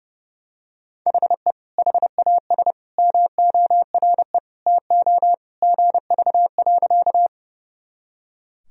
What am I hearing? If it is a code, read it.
Morse code, heard as a keyed tone, about 30 wpm, 700 Hz. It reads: HE HAS MORE TO GV.
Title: Sentences from Top 200 Words and Abbreviations 30wpm